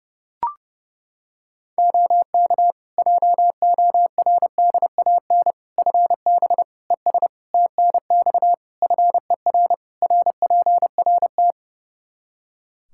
Morse code, heard as a keyed tone, about 30 wpm, 700 Hz. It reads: OK JORDAN FB ES TNX FER RPRT
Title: Practice QSO - Exchange 2 - Info - 30wpm